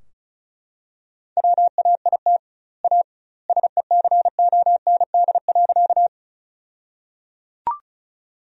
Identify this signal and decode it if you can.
Morse code, heard as a keyed tone, about 34 wpm, 700 Hz. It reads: WAIT A SECOND.
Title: 2x Word-Spacing - Sentences from Top 500 Words - 35wpm